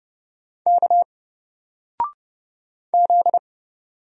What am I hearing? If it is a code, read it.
Morse code, heard as a keyed tone, about 30 wpm, 700 Hz. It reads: K Z
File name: Single-Letter-Number-30wpm